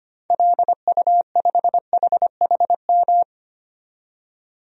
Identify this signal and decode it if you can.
Morse code, heard as a keyed tone, about 25 wpm, 700 Hz. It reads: LU5HHM
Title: DXCC-100 Call Signs - 6 Characters - 25wpm